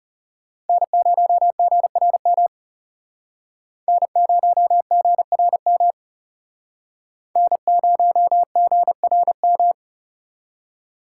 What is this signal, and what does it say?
Morse code, heard as a keyed tone, about 36 wpm, 700 Hz. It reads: N0GRM N0GRM N0GRM
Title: Speed Racing - USA 1x3 Call Signs 30wpm